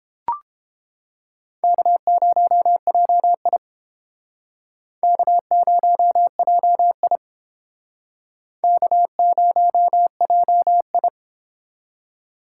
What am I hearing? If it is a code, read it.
Morse code, heard as a keyed tone, about 31 wpm, 700 Hz. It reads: K0JI K0JI K0JI